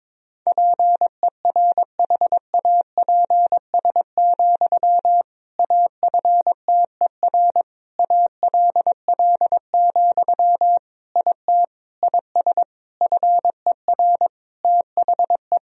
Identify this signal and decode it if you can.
Morse code, heard as a keyed tone, about 22 wpm, 700 Hz. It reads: PERHAPS, AFTER ALL, IT IS FER THE
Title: Sentences from Top 500 Words and Abbreviations 22wpm